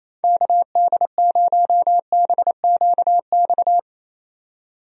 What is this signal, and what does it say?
Morse code, heard as a keyed tone, about 28 wpm, 700 Hz. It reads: KD0BQX
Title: USA Call Signs - 6 Characters - 28wpm